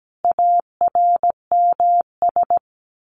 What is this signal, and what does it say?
Morse code, heard as a keyed tone, about 17 wpm, 700 Hz. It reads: ARMS
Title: A Sound Of Thunder 17wpm